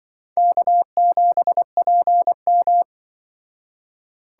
Morse code, heard as a keyed tone, about 25 wpm, 700 Hz. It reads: K7PM